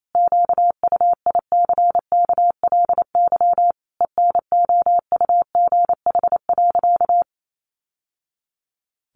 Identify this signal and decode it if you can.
Morse code, heard as a keyed tone, about 28 wpm, 700 Hz. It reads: QUICKLY ENOUGH.
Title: Sentences from Top 700 Words 28wpm